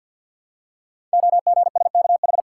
Morse code, heard as a keyed tone, about 50 wpm, 700 Hz. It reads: OGSKH
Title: Code Groups - 5 Letters - 50wpm